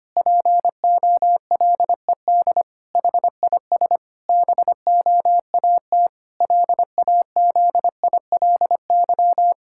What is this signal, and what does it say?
Morse code, heard as a keyed tone, about 25 wpm, 700 Hz. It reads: POLED HIS BOAT LAZILY
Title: Pangram Challenge v2 - 25wpm